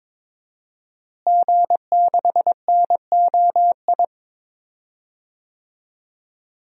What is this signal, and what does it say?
Morse code, heard as a keyed tone, about 21 wpm, 700 Hz. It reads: G6NOI
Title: DXCC-100 Call Signs - 5 Characters - Repeated 3x - 22wpm